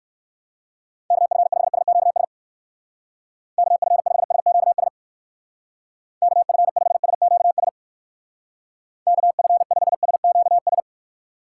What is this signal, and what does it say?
Morse code, heard as a keyed tone, about 57 wpm, 700 Hz. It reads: KF5SQS KF5SQS KF5SQS KF5SQS